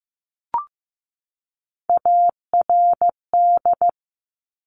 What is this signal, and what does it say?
Morse code, heard as a keyed tone, about 15 wpm, 700 Hz. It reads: ARD
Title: Top Three Letter Combinations in Words 15wpm